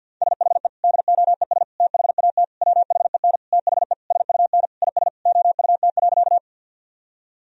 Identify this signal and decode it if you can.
Morse code, heard as a keyed tone, about 52 wpm, 700 Hz. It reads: SHE DOES THAT WHEN THE SUN IS OUT.
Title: Sentences from Top 1000 Words - Review Part 1 - 50wpm